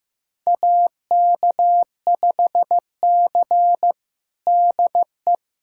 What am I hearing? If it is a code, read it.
Morse code, heard as a keyed tone, about 16 wpm, 700 Hz. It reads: AK5C DE
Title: USA Callsign DE Callsign - 4 Characters - 15wpm